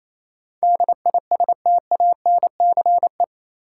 Morse code, heard as a keyed tone, about 28 wpm, 700 Hz. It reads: DISTANCE
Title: Sentences from Top 1200 Words - New Word Drill Alt - 28wpm